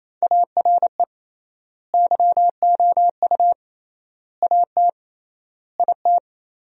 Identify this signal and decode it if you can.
Morse code, heard as a keyed tone, about 28 wpm, 700 Hz. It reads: ARE YOU AT IT
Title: 3x Word-Spacing - Sentences from Top 200 Words - 28wpm